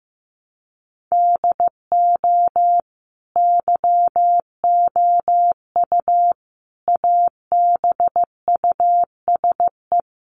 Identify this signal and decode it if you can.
Morse code, heard as a keyed tone, about 15 wpm, 700 Hz. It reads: DO YOU ABUSE
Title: Sentences from Top 1300 Words 15wpm